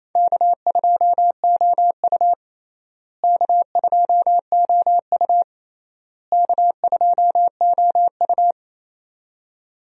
Morse code, heard as a keyed tone, about 28 wpm, 700 Hz. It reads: K2OU K2OU K2OU
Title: USA Call Signs - 4 Characters - Repeated 3x - 28wpm